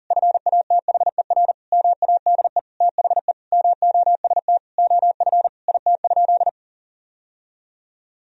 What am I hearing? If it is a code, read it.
Morse code, heard as a keyed tone, about 40 wpm, 700 Hz. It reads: FATHER MADE THE MOST OF IT?
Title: Sentences from Top 1000 Words - Review Part 1 - 40wpm